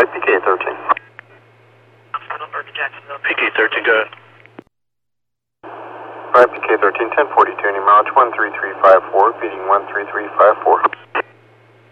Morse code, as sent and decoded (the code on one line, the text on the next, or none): none